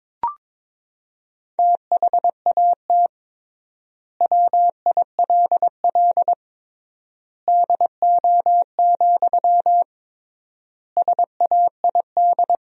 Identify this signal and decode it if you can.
Morse code, heard as a keyed tone, about 22 wpm, 700 Hz. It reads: THAT WILL DO, SAID
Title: 3x Word-Spacing - Sentences from Top 100 Words - 22wpm